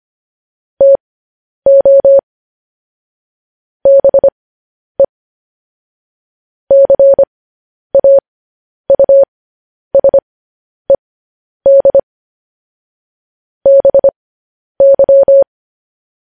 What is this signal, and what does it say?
Morse code, heard as a keyed tone, about 25 wpm, spaced as for 10 wpm, 550 Hz. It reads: TO BE CAUSED BY
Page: news-headlines-in-morse-code-at-10-wpm